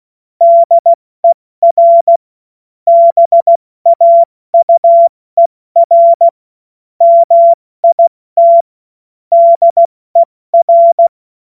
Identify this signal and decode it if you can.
Morse code, heard as a keyed tone, about 16 wpm, 680 Hz. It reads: DER BAUER MIT DER
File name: Morsekurs / Witze / Witz 24 Juni Mai 2025 70 F80